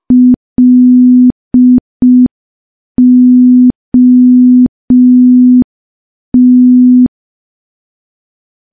Morse code, heard as a keyed tone, about 5 wpm, 250 Hz. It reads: LOT
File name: morse / morse-training-05WPM